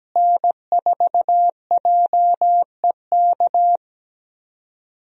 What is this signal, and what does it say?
Morse code, heard as a keyed tone, about 17 wpm, 700 Hz. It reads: N4JEK
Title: USA Call Signs - 5 Characters - Repeated 2x - 17wpm